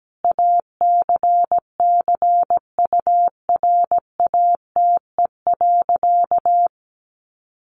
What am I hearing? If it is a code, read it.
Morse code, heard as a keyed tone, about 17 wpm, 700 Hz. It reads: ACCURATE.
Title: Sentences from Top 1200 Words 17wpm